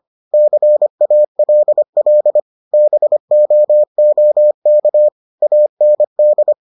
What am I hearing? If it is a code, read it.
Morse code, heard as a keyed tone, about 25 wpm, 600 Hz. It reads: CALL BOOK AND